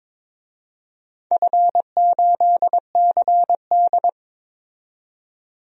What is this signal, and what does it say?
Morse code, heard as a keyed tone, about 22 wpm, 700 Hz. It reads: F8CD
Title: DXCC-100 Call Signs - 4 Characters - 22wpm